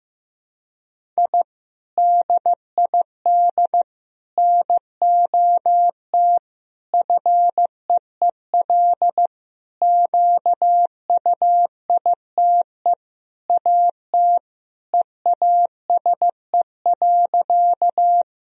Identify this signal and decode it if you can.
Morse code, heard as a keyed tone, about 15 wpm, 700 Hz. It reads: I DID NOT FEEL QUITE AT EASE.
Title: Sentences from Top 1000 Words - Review Part 1 - 15wpm